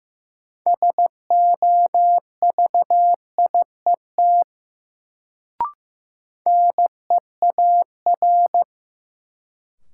Morse code, heard as a keyed tone, about 15 wpm, 700 Hz. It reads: SOVIET NEAR
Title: Sets of 5 Words Encore - New Word Drill - 15wpm